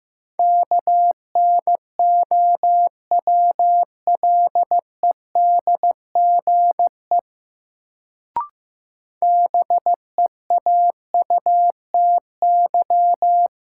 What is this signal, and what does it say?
Morse code, heard as a keyed tone, about 15 wpm, 700 Hz. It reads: KNOWLEDGE BEAUTY
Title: Sentences from Top 600 Words - New Word Drill - 15wpm